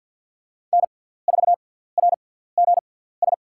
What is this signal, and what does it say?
Morse code, heard as a keyed tone, about 53 wpm, spaced as for 17 wpm, 700 Hz. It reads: N4RGS
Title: USA Call Signs - 5 Characters - 6x Character Spacing - 50wpm